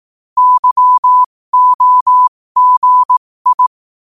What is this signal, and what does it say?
Morse code, heard as a keyed tone, about 18 wpm, spaced as for 15 wpm, 1000 Hz. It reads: YOGI
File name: qotdfiles / 15wpm